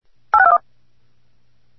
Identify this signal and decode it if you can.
DTMF, heard as telephone touch tones, tones of about 70 ms. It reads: 8621